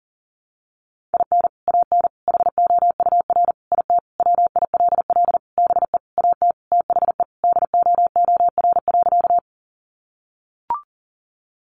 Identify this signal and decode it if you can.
Morse code, heard as a keyed tone, about 40 wpm, 700 Hz. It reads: IN AN HOUR IT WILL BE AT THE DOOR.
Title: Sentences from Top 400 Words 40wpm